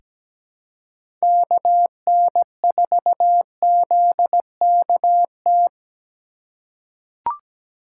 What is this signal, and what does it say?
Morse code, heard as a keyed tone, about 17 wpm, 700 Hz. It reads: KN4ZKT
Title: K1USN SST QSO Practice - 17wpm